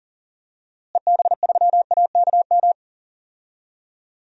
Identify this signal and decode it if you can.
Morse code, heard as a keyed tone, about 37 wpm, 700 Hz. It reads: EB3AKM
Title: DXCC-100 Call Signs - 6 Characters - Repeated 3x - 40wpm